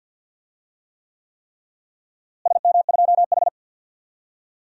Morse code, heard as a keyed tone, about 51 wpm, 700 Hz. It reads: SM2H
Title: DXCC-100 Call Signs - 4 Characters - Repeated 3x - 50wpm